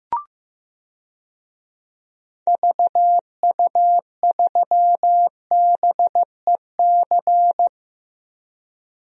Morse code, heard as a keyed tone, about 15 wpm, 700 Hz. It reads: VU3BEC
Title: DXCC-100 Call Signs - 6 Characters - Repeated 2x - 15wpm